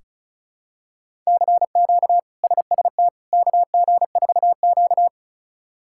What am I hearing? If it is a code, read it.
Morse code, heard as a keyed tone, about 36 wpm, 700 Hz. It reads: CQ SST KG4Q
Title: K1USN SST QSO Practice - 35wpm